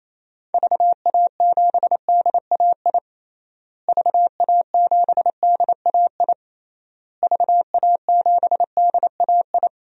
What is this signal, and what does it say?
Morse code, heard as a keyed tone, about 28 wpm, 700 Hz. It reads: VA7DAI VA7DAI VA7DAI
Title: DXCC-100 Call Signs - 6 Characters - Repeated 3x - 28wpm